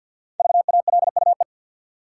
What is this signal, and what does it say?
Morse code, heard as a keyed tone, about 53 wpm, 700 Hz. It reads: VALUE
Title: Five Letter Words - 50wpm